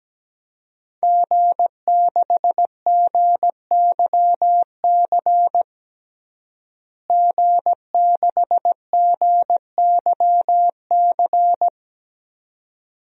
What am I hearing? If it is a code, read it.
Morse code, heard as a keyed tone, about 17 wpm, 700 Hz. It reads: G6GYC G6GYC
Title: DXCC-100 Call Signs - 5 Characters - Repeated 2x - 17wpm